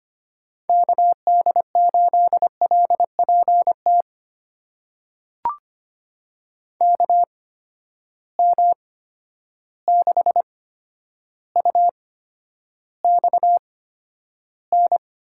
Morse code, heard as a keyed tone, about 25 wpm, spaced as for 11 wpm, 700 Hz. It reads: KD8LPT KM6UXN
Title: USA Call Signs - 6 Characters - 8x Character Spacing - 25wpm